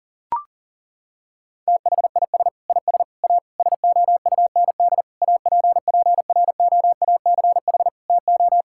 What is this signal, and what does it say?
Morse code, heard as a keyed tone, about 40 wpm, 700 Hz. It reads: THIS IS A SOUND APPROACH TO
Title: Sentences from Top 1000 Words - Review Part 6 - 40wpm